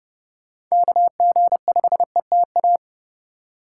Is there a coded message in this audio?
Yes: Morse code from 0.7 to 2.8 s, heard as a keyed tone, about 39 wpm, 700 Hz. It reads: KG5ETA